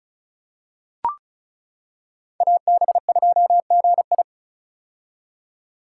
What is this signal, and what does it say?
Morse code, heard as a keyed tone, about 36 wpm, 700 Hz. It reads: AB2GI
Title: K1USN SST QSO Practice - 35wpm